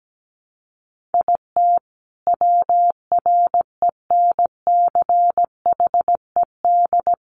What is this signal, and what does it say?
Morse code, heard as a keyed tone, about 17 wpm, 700 Hz. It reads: IT WRENCHED
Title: A Sound Of Thunder 17wpm